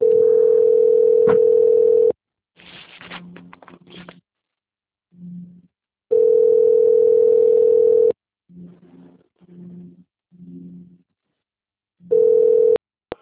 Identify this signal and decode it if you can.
Ringback tone (North America), 440 + 480 Hz.
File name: out- - - - - .53304